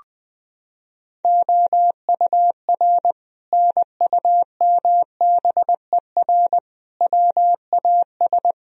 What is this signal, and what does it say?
Morse code, heard as a keyed tone, about 21 wpm, 700 Hz. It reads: OUR NUMBER WAS
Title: Sentences from Top 1000 Words - Review Part 2 - 20wpm